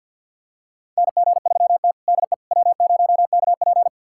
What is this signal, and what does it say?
Morse code, heard as a keyed tone, about 52 wpm, 700 Hz. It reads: NG3T DE W0KP
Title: USA Callsign DE Callsign - 4 Characters - 50wpm